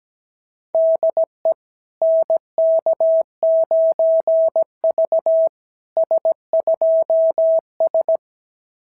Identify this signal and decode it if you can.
Morse code, heard as a keyed tone, about 17 wpm, 660 Hz. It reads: DE NK9V S2S
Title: SOTA QSO Practice 17wpm